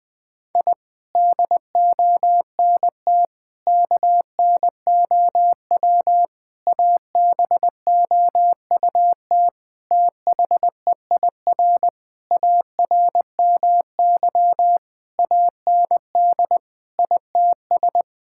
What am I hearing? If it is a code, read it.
Morse code, heard as a keyed tone, about 20 wpm, 700 Hz. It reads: I DONT KNOW ABOUT THEIR ARMY AND ITS
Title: Sentences from Top 1000 Words - Review Part 5 - 20wpm